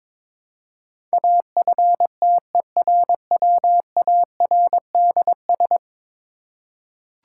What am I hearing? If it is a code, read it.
Morse code, heard as a keyed tone, about 22 wpm, 700 Hz. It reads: AFTERWARDS
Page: morse-code-ninja-practice-22wpm